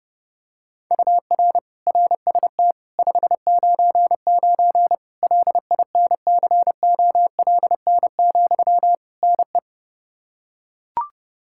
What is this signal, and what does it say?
Morse code, heard as a keyed tone, about 30 wpm, 700 Hz. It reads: UR RST 599 LINCOLN, NE
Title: Practice QSO - Exchange 1 - Beginning - 30wpm